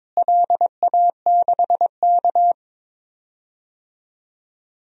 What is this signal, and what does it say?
Morse code, heard as a keyed tone, about 22 wpm, 700 Hz. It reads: LA6K
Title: DXCC-100 Call Signs - 4 Characters - Repeated 2x - 22wpm